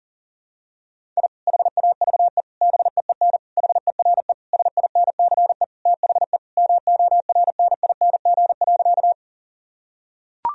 Morse code, heard as a keyed tone, about 40 wpm, 700 Hz. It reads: I HAVE BEEN HERE SINCE THE MORNING.
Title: Sentences from Top 1000 Words - Review Part 1 - 40wpm